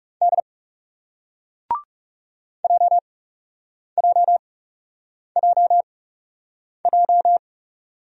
Morse code, heard as a keyed tone, about 40 wpm, spaced as for 8 wpm, 700 Hz. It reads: D JJJJ